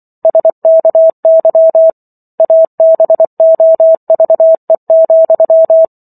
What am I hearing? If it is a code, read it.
Morse code, heard as a keyed tone, about 24 wpm, 650 Hz. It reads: SKY ABOVE,